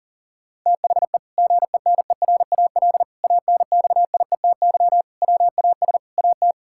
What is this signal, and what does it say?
Morse code, heard as a keyed tone, about 40 wpm, 700 Hz. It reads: THE GENERAL ANXIETY WAS AT
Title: Sentences from Top 1600 Words 40wpm